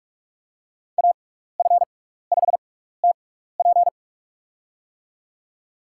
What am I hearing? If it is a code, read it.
Morse code, heard as a keyed tone, about 46 wpm, spaced as for 16 wpm, 700 Hz. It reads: AF5TP
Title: USA Call Signs - 5 Characters - 6x Character Spacing - 45wpm